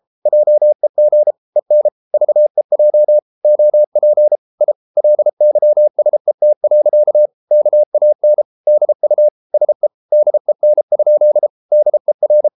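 Morse code, heard as a keyed tone, about 33 wpm, 600 Hz. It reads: JEG EN VEJ OP I LYSET. KAN DU SE DEN? DER
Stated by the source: Morse code at 33 wpm, QRQ.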